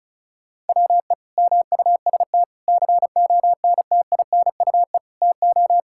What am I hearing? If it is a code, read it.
Morse code, heard as a keyed tone, about 36 wpm, 690 Hz. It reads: WE MUST CONTINUE TO
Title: Sentences from Top 1000 Words - Review Part 6 - 35wpm